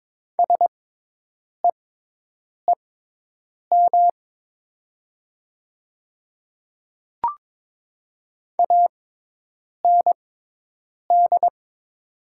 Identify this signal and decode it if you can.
Morse code, heard as a keyed tone, about 22 wpm, spaced as for 8 wpm, 700 Hz. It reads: SEEM AND
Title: 6x Character Spacing - Top 300 Words - 22wpm